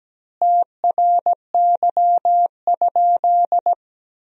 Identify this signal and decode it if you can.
Morse code, heard as a keyed tone, about 17 wpm, 700 Hz. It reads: TRY?
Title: 3x Word-Spacing - Sentences from Top 200 Words - 17wpm